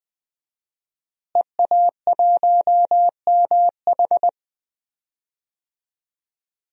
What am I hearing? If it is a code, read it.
Morse code, heard as a keyed tone, about 20 wpm, 700 Hz. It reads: EA1MH
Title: DXCC-100 Call Signs - 5 Characters - Repeated 3x - 20wpm